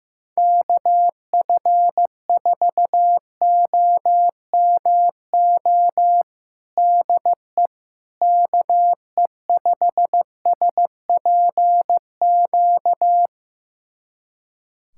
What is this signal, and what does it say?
Morse code, heard as a keyed tone, about 15 wpm, 700 Hz. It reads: KF4OMO DE KE5SPQ
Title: USA Callsign DE Callsign - 6 Characters - 15wpm